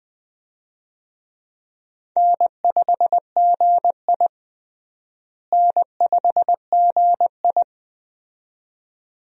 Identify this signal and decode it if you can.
Morse code, heard as a keyed tone, about 20 wpm, 700 Hz. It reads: N5GI N5GI
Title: USA Call Signs - 4 Characters - Repeated 2x - 20wpm